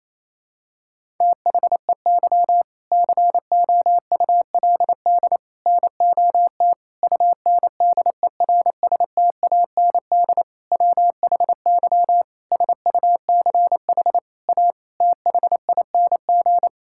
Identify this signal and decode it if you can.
Morse code, heard as a keyed tone, about 28 wpm, 700 Hz. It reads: THEY COULD NOT UNDERSTAND WHY SUCH A THING
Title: Sentences from Top 1000 Words - Review Part 6 - 28wpm